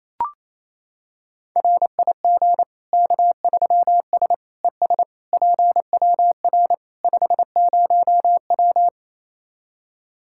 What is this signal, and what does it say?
Morse code, heard as a keyed tone, about 28 wpm, 700 Hz. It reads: RIG K3S ES PWR 50W
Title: K1USN SST QSO Practice 28wpm